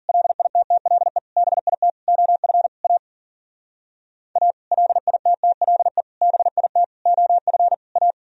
Morse code, heard as a keyed tone, about 46 wpm, 700 Hz. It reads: LITTLE BIT OF A A LITTLE BIT OF A
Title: Speed-Racing - Sets of 5 Words 40wpm